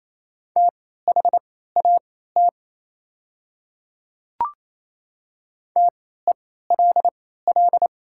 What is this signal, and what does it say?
Morse code, heard as a keyed tone, about 27 wpm, spaced as for 16 wpm, 700 Hz. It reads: THAT TELL
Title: 3x Character Spacing - Top 200 Words - 28wpm